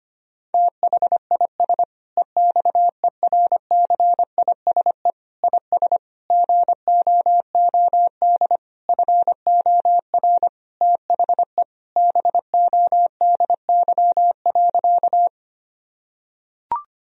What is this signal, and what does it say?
Morse code, heard as a keyed tone, about 25 wpm, 700 Hz. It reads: THIS EXERCISE IS GOOD FOR THE BODY.
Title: Sentences from Top 1000 Words - Review Part 4 - 25wpm